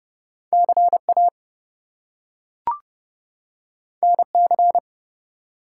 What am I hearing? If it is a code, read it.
Morse code, heard as a keyed tone, about 30 wpm, 700 Hz. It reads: CA NC
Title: US States and Canadian Provinces 30wpm